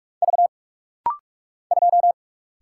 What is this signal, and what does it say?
Morse code, heard as a keyed tone, about 46 wpm, 700 Hz. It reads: V 2
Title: Single Letter-Number - Rapid-Fire 45wpm